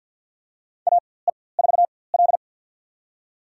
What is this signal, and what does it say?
Morse code, heard as a keyed tone, about 46 wpm, spaced as for 23 wpm, 700 Hz. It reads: AE4L
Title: USA Call Signs - 4x Character Spacing - 50wpm